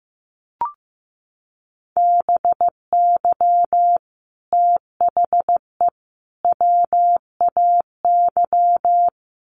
Morse code, heard as a keyed tone, about 15 wpm, 700 Hz. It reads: BY THE WAY
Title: Sets of 3 Words 15wpm